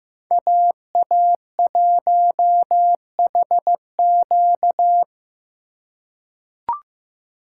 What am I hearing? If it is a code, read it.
Morse code, heard as a keyed tone, about 15 wpm, 700 Hz. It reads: AA1HQ
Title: USA Call Signs - 5 Characters - 3x Character Spacing - 15wpm